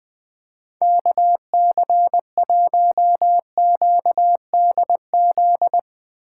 Morse code, heard as a keyed tone, about 20 wpm, 700 Hz. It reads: KC1QDZ
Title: 2023 POTA Hunter Call Signs - 20wpm